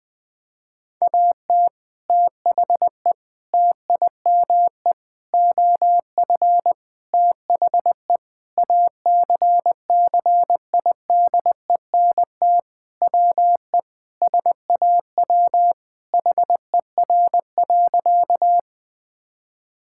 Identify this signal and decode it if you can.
Morse code, heard as a keyed tone, about 20 wpm, 700 Hz. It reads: AT THE TIME OF THE ACCIDENT WE SAW HER.
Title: Sentences with 4-Word Phrases - 20wpm